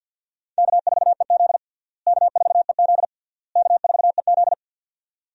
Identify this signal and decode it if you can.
Morse code, heard as a keyed tone, about 46 wpm, 700 Hz. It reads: K4EZ K4EZ K4EZ
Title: USA Call Signs - 4 Characters - Repeated 3x - 50wpm